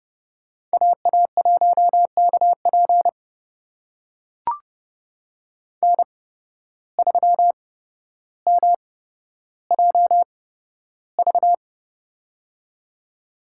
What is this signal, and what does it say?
Morse code, heard as a keyed tone, about 30 wpm, spaced as for 13 wpm, 700 Hz. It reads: AA1KP N3MJV
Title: USA Call Signs - 5 Characters - 8x Character Spacing - 30wpm